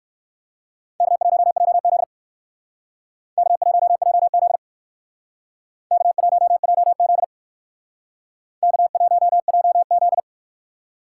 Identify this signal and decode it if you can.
Morse code, heard as a keyed tone, about 57 wpm, 700 Hz. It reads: K1JZ K1JZ K1JZ K1JZ